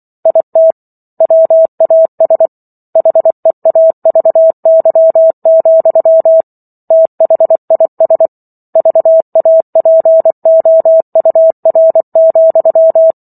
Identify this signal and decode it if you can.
Morse code, heard as a keyed tone, about 24 wpm, 650 Hz. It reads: IT WAS HEAVY, THIS VAPOUR,